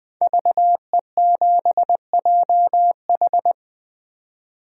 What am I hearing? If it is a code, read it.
Morse code, heard as a keyed tone, about 20 wpm, 700 Hz. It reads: VE7JH
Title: 2023 SOTA Chaser Call Signs - 20wpm